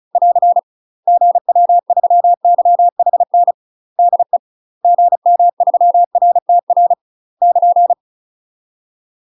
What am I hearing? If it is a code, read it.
Morse code, heard as a keyed tone, about 36 wpm, 700 Hz. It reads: <AR> GW3YHN DE GM3RTR <KN>
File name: UK MASTER QSOs TRAINING/35WPM